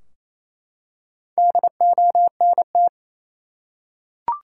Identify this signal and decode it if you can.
Morse code, heard as a keyed tone, about 27 wpm, 700 Hz. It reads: DONT
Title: Top 180 QSO Elements 28wpm